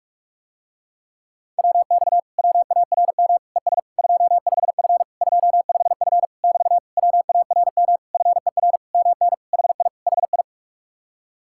Morse code, heard as a keyed tone, about 49 wpm, 700 Hz. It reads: WX WARM ES 25F 25F <BT> WARM FER MN HI HI